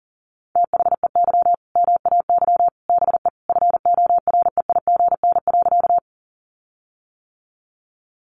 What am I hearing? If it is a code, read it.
Morse code, heard as a keyed tone, about 37 wpm, 700 Hz. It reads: THEY MAY BE FOREIGN.
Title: Sentences from Top 1400 Words 40wpm